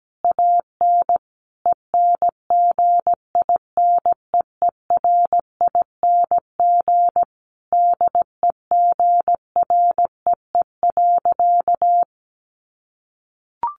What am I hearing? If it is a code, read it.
Morse code, heard as a keyed tone, about 17 wpm, 700 Hz. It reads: AN ENGINEERING DEGREE.
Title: Sentences from Top 1500 Words 17wpm